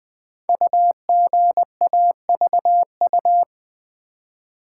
Morse code, heard as a keyed tone, about 20 wpm, 700 Hz. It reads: UGAVU
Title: Code Groups - 5 Letter:Numbers - 20wpm